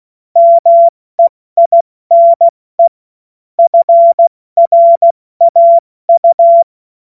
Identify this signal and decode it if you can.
Morse code, heard as a keyed tone, about 16 wpm, 680 Hz. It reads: MEINE FRAU
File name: Morsekurs / Witze / Witz 17.Oktober 2023 70 F80